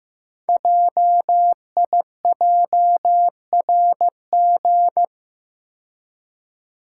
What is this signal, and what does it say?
Morse code, heard as a keyed tone, about 15 wpm, 700 Hz. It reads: JIJRG
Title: Code Groups - 5 Letters - 15wpm